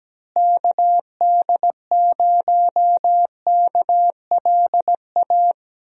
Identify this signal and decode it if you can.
Morse code, heard as a keyed tone, about 17 wpm, 700 Hz. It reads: KD0KLA
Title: USA Call Signs - 6 Characters - Repeated 3x - 17wpm